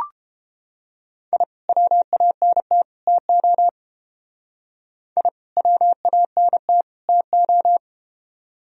Morse code, heard as a keyed tone, about 31 wpm, 700 Hz. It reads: I WANT TO I WANT TO